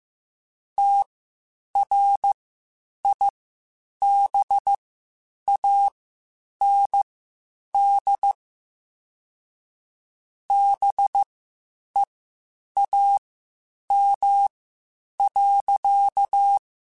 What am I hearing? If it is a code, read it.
Morse code, heard as a keyed tone, about 15 wpm, spaced as for 8 wpm, 800 Hz. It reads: TRIBAND BEAM.
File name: QSOs / QSOs-mp3s / WD0BGP 1@10